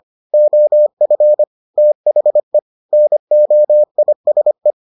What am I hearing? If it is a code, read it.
Morse code, heard as a keyed tone, about 25 wpm, 600 Hz. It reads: OF THE NOISE